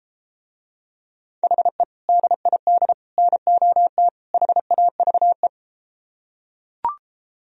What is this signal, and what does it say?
Morse code, heard as a keyed tone, about 33 wpm, 700 Hz. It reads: HE DID NOT HAVE